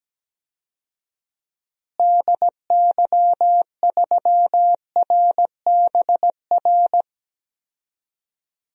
Morse code, heard as a keyed tone, about 17 wpm, 700 Hz. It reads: DY3RBR
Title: DXCC-100 Call Signs - 6 Characters - Repeated 2x - 17wpm